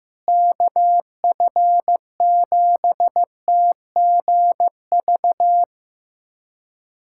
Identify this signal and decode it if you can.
Morse code, heard as a keyed tone, about 15 wpm, 700 Hz. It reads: KF7TGV
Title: USA Callsign DE Callsign - 6 Characters - 15wpm